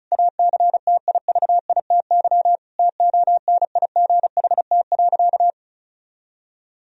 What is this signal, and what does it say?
Morse code, heard as a keyed tone, about 36 wpm, 700 Hz. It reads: ACTIVITY TONIGHT.
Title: Sentences from Top 1000 Words - Review Part 4 - 35wpm